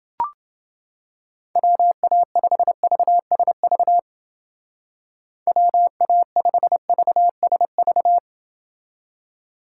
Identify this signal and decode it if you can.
Morse code, heard as a keyed tone, about 29 wpm, 700 Hz. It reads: WA5VSV WA5VSV